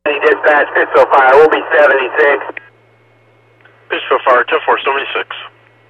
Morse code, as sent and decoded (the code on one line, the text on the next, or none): none